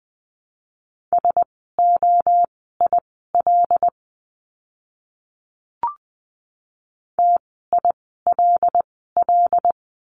Morse code, heard as a keyed tone, about 20 wpm, spaced as for 14 wpm, 700 Hz. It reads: SOIL TILL
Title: 075 Lesson 13f - Letter L - Words 20wpm